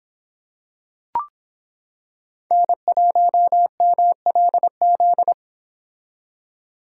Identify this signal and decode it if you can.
Morse code, heard as a keyed tone, about 26 wpm, 700 Hz. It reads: N1MLZ